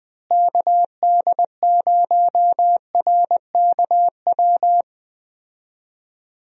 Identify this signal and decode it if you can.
Morse code, heard as a keyed tone, about 20 wpm, 700 Hz. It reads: KD0RKW
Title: USA Call Signs - 6 Characters - Repeated 3x - 20wpm